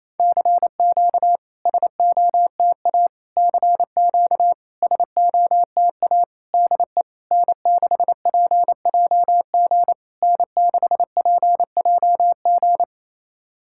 Morse code, heard as a keyed tone, about 28 wpm, 690 Hz. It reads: CQ SOTA CQ SOTA DE N6PJG N6PJG
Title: SOTA QSO Practice 28wpm